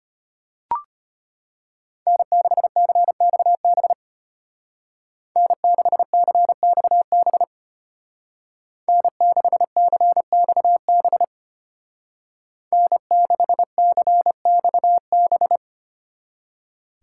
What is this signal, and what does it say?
Morse code, heard as a keyed tone, about 32 wpm, 700 Hz. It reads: N6CXB N6CXB N6CXB N6CXB